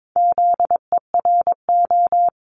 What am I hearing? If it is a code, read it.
Morse code, heard as a keyed tone, about 22 wpm, 700 Hz. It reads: ZERO
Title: Multi-letter Phonemes Repeated 3x 22wpm